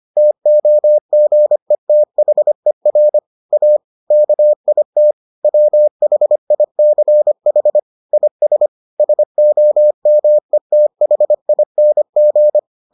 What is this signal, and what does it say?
Morse code, heard as a keyed tone, about 25 wpm, 600 Hz. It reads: TOGETHER A KIT WHICH IS SOMETHING